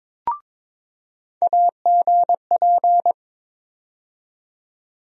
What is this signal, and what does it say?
Morse code, heard as a keyed tone, about 22 wpm, 700 Hz. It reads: AGP
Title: USA Call Sign Postfixes - 3 Letters 22wpm